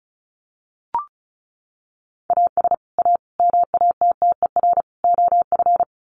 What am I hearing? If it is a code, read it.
Morse code, heard as a keyed tone, about 34 wpm, 700 Hz. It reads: AS A MATTER OF
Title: Sets of 4 Words 35wpm